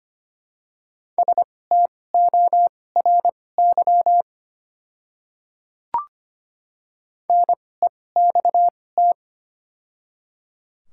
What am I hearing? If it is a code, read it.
Morse code, heard as a keyed tone, about 25 wpm, spaced as for 18 wpm, 690 Hz. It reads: STORY NEXT
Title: 2x Character Spacing - Top 300 Words - 25wpm